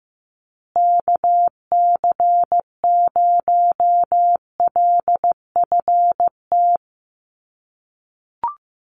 Morse code, heard as a keyed tone, about 15 wpm, 700 Hz. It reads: KC0LFT
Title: USA Call Signs Repeated 2x 15wpm